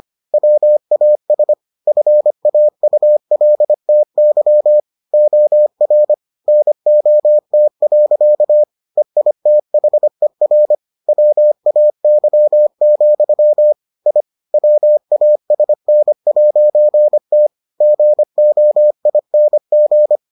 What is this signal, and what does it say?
Morse code, heard as a keyed tone, about 25 wpm, 600 Hz. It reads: WAS FAULTY OR NOT. EITHER WAY, I WASN'T GOING